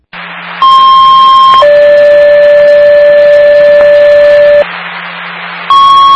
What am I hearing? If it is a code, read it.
Two-tone page, heard as a one-second tone, then a three-second tone, about 1060 then 580 Hz. It is heard from 0.6 to 4.6 s.